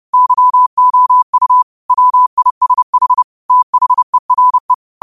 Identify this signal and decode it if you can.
Morse code, heard as a keyed tone, about 30 wpm, 1000 Hz. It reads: YOU WISH THERE
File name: qotdfiles / 30wpm